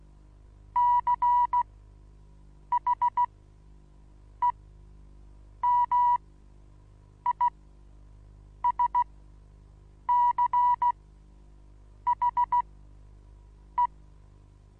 Morse code, heard as a keyed tone, about 16 wpm, spaced as for 6 wpm, 970 Hz. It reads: CHEMISCHE
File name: Onlinelehrgang / morsen / DJ4UF Morsen25